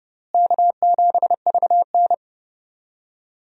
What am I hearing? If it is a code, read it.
Morse code, heard as a keyed tone, about 30 wpm, 700 Hz. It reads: K7VN
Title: USA Call Signs - 4 Characters - Repeated 3x - 30wpm